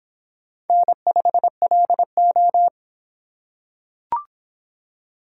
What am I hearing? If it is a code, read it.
Morse code, heard as a keyed tone, about 25 wpm, 700 Hz. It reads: N5LO